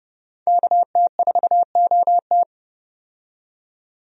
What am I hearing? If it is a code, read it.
Morse code, heard as a keyed tone, about 32 wpm, 700 Hz. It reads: KT4OT
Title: USA Call Signs - 5 Characters - 30wpm